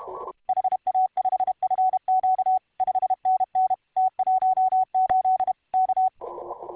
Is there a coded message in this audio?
Yes: Morse code from 0.5 to 6.1 s, heard as a keyed tone, about 31 wpm, 750 Hz. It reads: HA5FQ 5NN T18 K